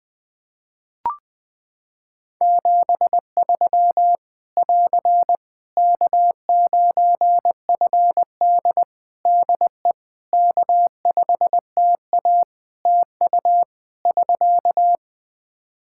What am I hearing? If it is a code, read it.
Morse code, heard as a keyed tone, about 20 wpm, 700 Hz. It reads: 73 <AR> K9FD DE K5TA TU <SK>
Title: K1USN SST QSO Practice 20wpm